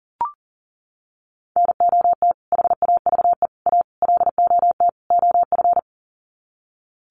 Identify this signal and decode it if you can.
Morse code, heard as a keyed tone, about 40 wpm, 700 Hz. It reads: NOT HAVE A LOT OF
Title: Sets of 5 Words 40wpm